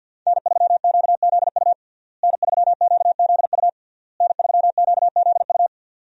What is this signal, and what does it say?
Morse code, heard as a keyed tone, about 52 wpm, 700 Hz. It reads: N3QZU N3QZU N3QZU
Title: USA Call Signs - 5 Characters - Repeated 3x - 50wpm